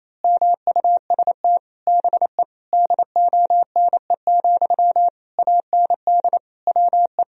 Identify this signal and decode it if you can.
Morse code, heard as a keyed tone, about 28 wpm, 700 Hz. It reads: MUST BE DONE, AND WE
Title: Sentences from Top 1000 Words - Review Part 1 - 28wpm